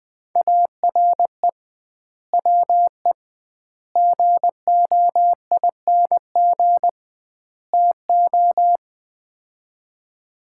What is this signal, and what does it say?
Morse code, heard as a keyed tone, about 20 wpm, 700 Hz. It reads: ARE WE GOING TO
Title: 2x Word-Spacing - Sets of 4 Words - 20wpm